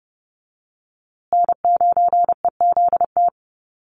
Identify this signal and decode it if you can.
Morse code, heard as a keyed tone, about 30 wpm, 700 Hz. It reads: N9EZT
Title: USA Call Signs Repeated 3x 30wpm